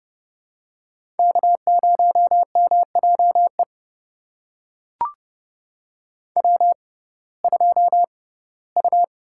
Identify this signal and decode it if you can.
Morse code, heard as a keyed tone, about 30 wpm, spaced as for 18 wpm, 700 Hz. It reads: K0MJE W2U
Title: USA Call Signs - 5 Characters - 6x Character Spacing - 30wpm